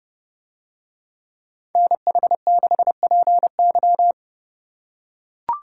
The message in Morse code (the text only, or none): NH6PY